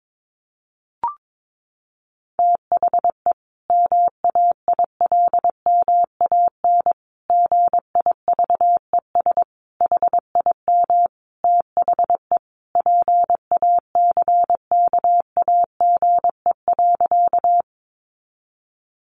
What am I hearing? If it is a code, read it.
Morse code, heard as a keyed tone, about 22 wpm, 700 Hz. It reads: THE MAILMAN GIVES HIM THE PACKAGE.
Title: Additional Easy Phrases 22wpm